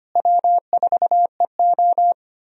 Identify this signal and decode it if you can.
Morse code, heard as a keyed tone, about 25 wpm, 700 Hz. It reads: W4EO
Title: USA Call Signs - 4 Characters - Repeated 3x - 25wpm